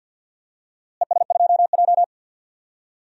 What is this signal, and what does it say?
Morse code, heard as a keyed tone, about 49 wpm, 700 Hz. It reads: ES2J
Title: DXCC-100 Call Signs - 4 Characters - Repeated 2x - 50wpm